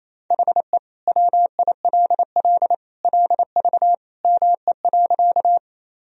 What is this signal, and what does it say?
Morse code, heard as a keyed tone, about 28 wpm, 700 Hz. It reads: HE WILL LV ME.
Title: Sentences from Top 400 Words and Abbreviations 28wpm